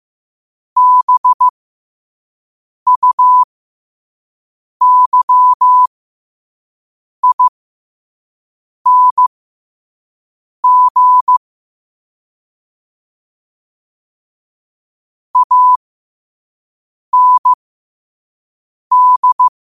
Morse code, heard as a keyed tone, about 15 wpm, spaced as for 5 wpm, 1000 Hz. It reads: BUYING AND
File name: qotdfiles / 5wpm